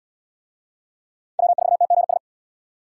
Morse code, heard as a keyed tone, about 68 wpm, 700 Hz. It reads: W4TGS